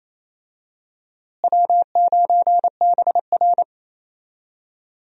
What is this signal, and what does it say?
Morse code, heard as a keyed tone, about 28 wpm, 700 Hz. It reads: W9BR
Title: USA Call Signs - 4 Characters - Repeated 3x - 28wpm